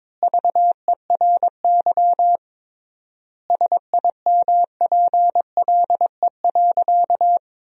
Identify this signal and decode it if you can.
Morse code, heard as a keyed tone, about 22 wpm, 700 Hz. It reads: VERY SIMPLE.
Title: 3x Word-Spacing - Sentences from Top 400 Words - 22wpm